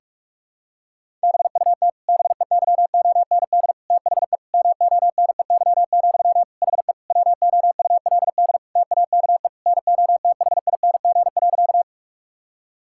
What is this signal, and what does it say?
Morse code, heard as a keyed tone, about 46 wpm, 700 Hz. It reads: BUT BEYOND THE MONEY, HE WOULD TAKE NOTHING.
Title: Sentences from Top 1000 Words - Review Part 6 - 45wpm